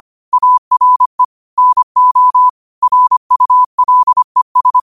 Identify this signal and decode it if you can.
Morse code, heard as a keyed tone, about 25 wpm, 1000 Hz. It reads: ARE NO RULES
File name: qotdfiles / 25wpm